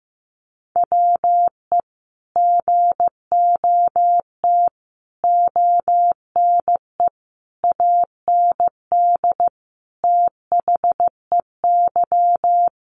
Morse code, heard as a keyed tone, about 15 wpm, 700 Hz. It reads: WE GOT ONE AND THEY
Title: Sentences from Top 300 Words 15wpm